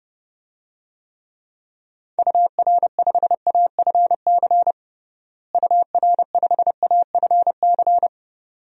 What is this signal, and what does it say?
Morse code, heard as a keyed tone, about 32 wpm, 700 Hz. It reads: UR5AFC UR5AFC
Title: DXCC-100 Call Signs - 6 Characters - Repeated 2x - 30wpm